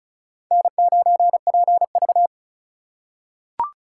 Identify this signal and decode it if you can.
Morse code, heard as a keyed tone, about 36 wpm, 700 Hz. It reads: N9PV
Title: USA Call Signs - 2x Character Spacing - 35wpm